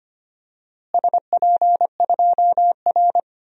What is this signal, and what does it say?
Morse code, heard as a keyed tone, about 25 wpm, 700 Hz. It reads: SP2R
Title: DXCC-100 Call Signs - 4 Characters - Repeated 2x - 25wpm